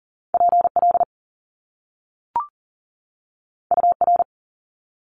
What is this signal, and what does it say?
Morse code, heard as a keyed tone, about 40 wpm, 700 Hz. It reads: PL UR
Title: Top Two Letter Combinations in Words 40wpm